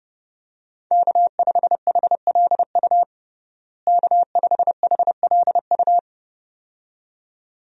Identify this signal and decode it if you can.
Morse code, heard as a keyed tone, about 30 wpm, 700 Hz. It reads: K5HLU K5HLU
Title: USA Call Signs - 5 Characters - Repeated 2x - 30wpm